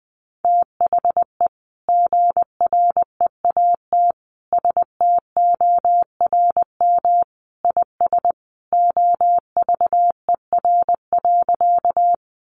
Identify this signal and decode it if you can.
Morse code, heard as a keyed tone, about 19 wpm, 700 Hz. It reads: THE GREAT STORM IS OVER.
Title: Sentences from Top 1500 Words 20wpm